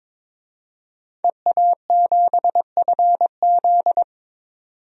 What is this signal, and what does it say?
Morse code, heard as a keyed tone, about 22 wpm, 700 Hz. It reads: EA7FZ
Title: DXCC-100 Call Signs - 5 Characters - 22wpm